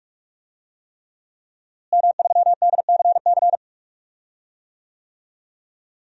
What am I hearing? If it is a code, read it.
Morse code, heard as a keyed tone, about 46 wpm, 700 Hz. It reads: M3DCC
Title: DXCC-100 Call Signs - 5 Characters - Repeated 3x - 45wpm